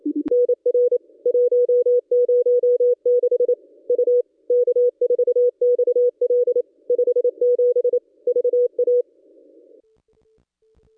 Morse code, heard as a keyed tone, about 28 wpm, 440 Hz. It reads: NR 106 U K4XL 57 VA
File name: contests / ss-cw-k5tr-op-n5rz -1100